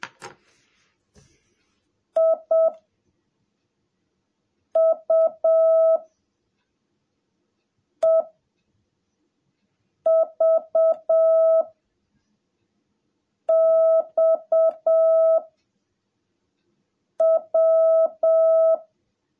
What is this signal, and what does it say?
Morse code, heard as a keyed tone, about 7 wpm, 660 Hz. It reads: I U E V X W
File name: jh8cbh / 2021cw / cw17a